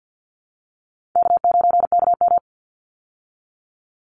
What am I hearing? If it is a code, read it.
Morse code, heard as a keyed tone, about 49 wpm, 700 Hz. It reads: K9KM
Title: Call Signs Repeated 3x - 50wpm